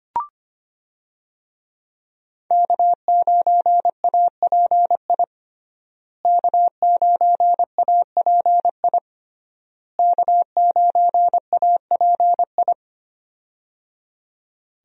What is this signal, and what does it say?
Morse code, heard as a keyed tone, about 25 wpm, 700 Hz. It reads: K9API K9API K9API
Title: USA Call Signs - 5 Characters - Repeated 3x - 25wpm